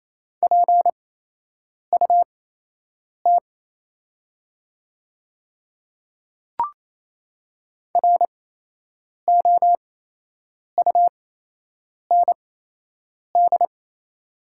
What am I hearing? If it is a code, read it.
Morse code, heard as a keyed tone, about 28 wpm, spaced as for 8 wpm, 700 Hz. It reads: PUT ROUND
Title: 8x Character Spacing - Top 200 Words - 28wpm